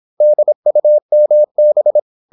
Morse code, heard as a keyed tone, about 26 wpm, 600 Hz. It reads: DUMB
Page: Never 26wpm | CW med Gnister